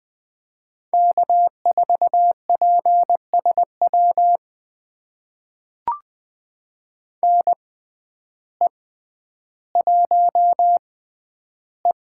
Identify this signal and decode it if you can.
Morse code, heard as a keyed tone, about 20 wpm, spaced as for 11 wpm, 700 Hz. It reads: K4PSW NE1E
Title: USA Call Signs - 5 Characters - 6x Character Spacing - 20wpm